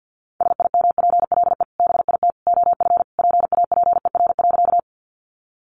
Morse code, heard as a keyed tone, about 50 wpm, 700 Hz. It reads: SIMPLE BIT OF PAPER.
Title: Sentences from Top 900 Words 50wpm